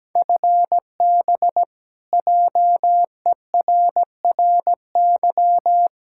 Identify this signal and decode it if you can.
Morse code, heard as a keyed tone, about 17 wpm, 700 Hz. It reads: FB JERRY
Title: K1USN SST QSO Practice 17wpm